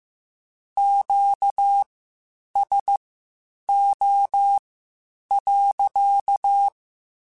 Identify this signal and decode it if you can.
Morse code, heard as a keyed tone, about 15 wpm, spaced as for 8 wpm, 800 Hz. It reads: QSO.
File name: Q Calls / Q Calls-mp3s / Q Calls sppc @13